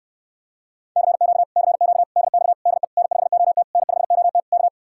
Morse code, heard as a keyed tone, about 69 wpm, 700 Hz. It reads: CQ CQ DX DE N5OT N5OT K